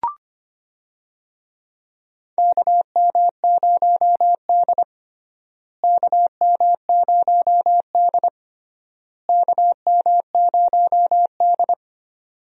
Morse code, heard as a keyed tone, about 25 wpm, 700 Hz. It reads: KM0D KM0D KM0D
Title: USA Call Signs - 4 Characters - Repeated 3x - 25wpm